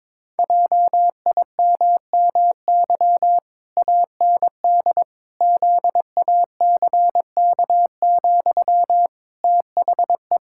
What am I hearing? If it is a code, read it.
Morse code, heard as a keyed tone, about 22 wpm, 700 Hz. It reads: JIMMY AND ZACK, THE
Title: Pangram Challenge v2 - 22wpm